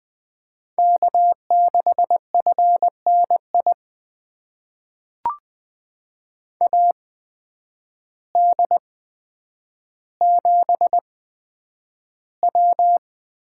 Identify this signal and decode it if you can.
Morse code, heard as a keyed tone, about 20 wpm, spaced as for 9 wpm, 700 Hz. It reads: K6FNI AD7W
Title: USA Call Signs - 8x Character Spacing - 20wpm